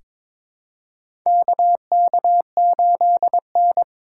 Morse code, heard as a keyed tone, about 22 wpm, 700 Hz. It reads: KK8N
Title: USA Call Signs - 4 Characters - 2x Character Spacing - 22wpm